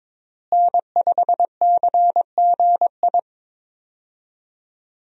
Morse code, heard as a keyed tone, about 22 wpm, 700 Hz. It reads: N5CGI
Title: USA Call Signs - 5 Characters - Repeated 2x - 22wpm